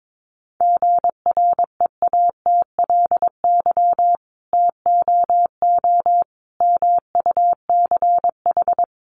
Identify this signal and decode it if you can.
Morse code, heard as a keyed tone, about 22 wpm, 700 Hz. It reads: GREATLY TOO MUCH
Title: Sentences from Top 1400 Words 22wpm